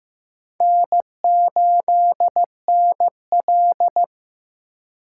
Morse code, heard as a keyed tone, about 18 wpm, 700 Hz. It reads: N8NL
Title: Speed Racing - USA 1x2 Call Signs 15wpm